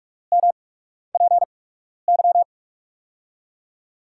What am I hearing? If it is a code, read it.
Morse code, heard as a keyed tone, about 43 wpm, spaced as for 12 wpm, 700 Hz. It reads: MPY
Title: USA Call Signs - 6 Characters - 8x Character Spacing - 45wpm